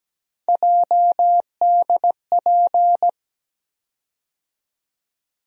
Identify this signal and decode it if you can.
Morse code, heard as a keyed tone, about 17 wpm, 700 Hz. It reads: JDP
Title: USA Call Sign Postfixes - 3 Letters 17wpm